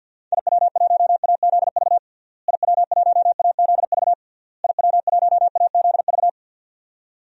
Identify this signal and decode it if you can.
Morse code, heard as a keyed tone, about 52 wpm, 700 Hz. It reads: IW1AZV IW1AZV IW1AZV
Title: DXCC-100 Call Signs - 6 Characters - Repeated 3x - 50wpm